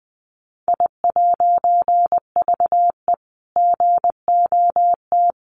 Morse code, heard as a keyed tone, about 20 wpm, 700 Hz. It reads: I'VE GOT
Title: A Sound Of Thunder 20wpm